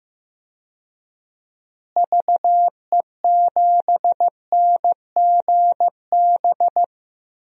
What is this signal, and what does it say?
Morse code, heard as a keyed tone, about 15 wpm, 700 Hz. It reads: VE7NGB
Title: DXCC-100 Call Signs - 6 Characters - Repeated 2x - 15wpm